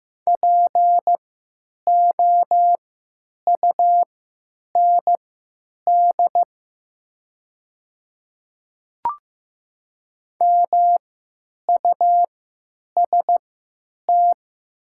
Morse code, heard as a keyed tone, about 15 wpm, spaced as for 9 wpm, 700 Hz. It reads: POUND MUST
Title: 3x Character Spacing - Top 500 Words - 15wpm